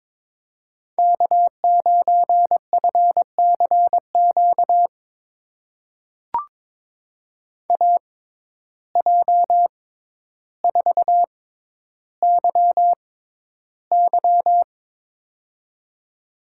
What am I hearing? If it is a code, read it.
Morse code, heard as a keyed tone, about 23 wpm, spaced as for 11 wpm, 700 Hz. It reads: K9FCQ AJ4YY
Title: USA Call Signs - 5 Characters - 6x Character Spacing - 22wpm